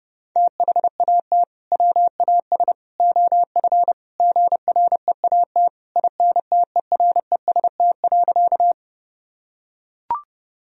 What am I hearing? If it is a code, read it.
Morse code, heard as a keyed tone, about 30 wpm, 700 Hz. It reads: THAT WAS OF GREAT INTEREST.
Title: Sentences from Top 1000 Words - Review Part 2 - 30wpm